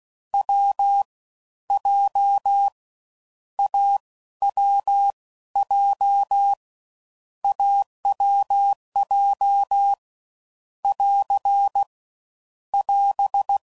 Morse code, heard as a keyed tone, about 16 wpm, spaced as for 13 wpm, 800 Hz. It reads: W J AWJ AWJ <AR> <AS>
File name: Combinations / Difficult Combos / AWJ